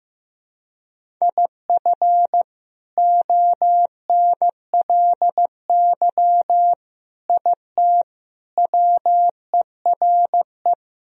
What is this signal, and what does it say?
Morse code, heard as a keyed tone, about 15 wpm, 700 Hz. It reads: IF ONLY IT WERE
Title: Sentences from Top 500 Words and Abbreviations 15wpm